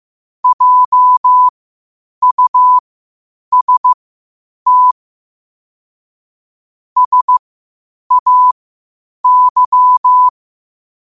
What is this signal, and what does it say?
Morse code, heard as a keyed tone, about 15 wpm, spaced as for 8 wpm, 1000 Hz. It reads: JUST SAY
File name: qotdfiles / 8wpm